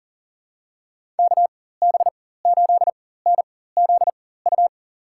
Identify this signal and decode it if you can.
Morse code, heard as a keyed tone, about 40 wpm, spaced as for 19 wpm, 700 Hz. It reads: KB8NZU
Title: USA Call Signs - 6 Characters - 4x Character Spacing - 40wpm